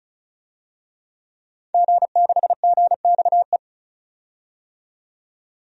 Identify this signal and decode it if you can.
Morse code, heard as a keyed tone, about 36 wpm, 700 Hz. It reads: G6GXE
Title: DXCC-100 Call Signs - 5 Characters - 35wpm